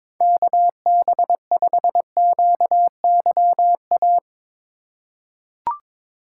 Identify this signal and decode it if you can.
Morse code, heard as a keyed tone, about 23 wpm, 700 Hz. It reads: KB5QYA
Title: USA Call Signs - 2x Character Spacing - 22wpm